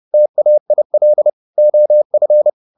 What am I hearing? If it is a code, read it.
Morse code, heard as a keyed tone, about 30 wpm, 600 Hz. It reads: TAIL OF